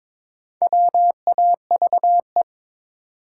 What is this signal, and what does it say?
Morse code, heard as a keyed tone, about 22 wpm, 700 Hz. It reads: WAVE
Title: Sentences from Top 1400 Words - New Word Drill Alt - 22wpm